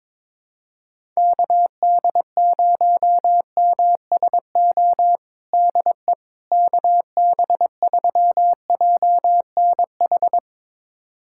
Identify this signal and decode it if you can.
Morse code, heard as a keyed tone, about 22 wpm, 700 Hz. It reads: KD0MSO DE KB3JNH
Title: USA Callsign DE Callsign - 6 Characters - 22wpm